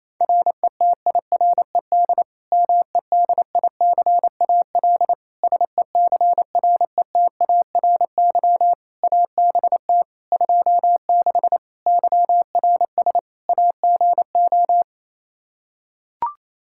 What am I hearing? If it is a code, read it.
Morse code, heard as a keyed tone, about 28 wpm, 700 Hz. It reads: RETIRED MEDICAL SECRETARY ABT 26 YRS AGO
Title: K1USN SST QSO Practice 28wpm